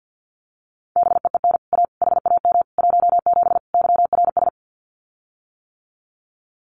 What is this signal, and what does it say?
Morse code, heard as a keyed tone, about 48 wpm, 700 Hz. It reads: BEEN A HAM 17 YRS
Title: Common QSO Phrases Encore 50wpm